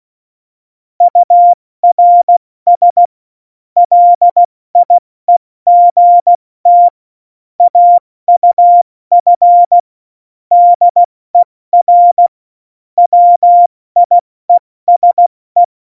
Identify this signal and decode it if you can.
Morse code, heard as a keyed tone, about 16 wpm, 700 Hz. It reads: URS LIEGT AUF DER WIESE
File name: Morsekurs / Witze / Witz 17. Oktober 17, Tempo 70